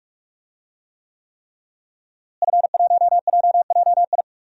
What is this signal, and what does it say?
Morse code, heard as a keyed tone, about 46 wpm, 700 Hz. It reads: F1JJI
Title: DXCC-100 Call Signs - 5 Characters - Repeated 3x - 45wpm